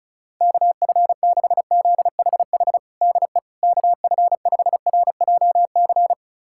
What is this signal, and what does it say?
Morse code, heard as a keyed tone, about 36 wpm, 700 Hz. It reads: KF6ZHH DE KF5RJC
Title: USA Callsign DE Callsign 35wpm